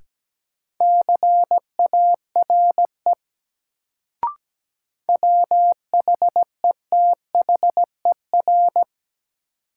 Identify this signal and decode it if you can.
Morse code, heard as a keyed tone, about 17 wpm, 700 Hz. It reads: CARE WHETHER
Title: Sets of 3 Words - New Word Drill - 17wpm